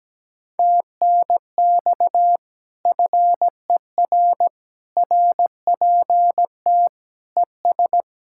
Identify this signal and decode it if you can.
Morse code, heard as a keyed tone, about 17 wpm, 700 Hz. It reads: TNX FER RPT ES
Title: Speed-Racing Common QSO Phrases Encore - 17wpm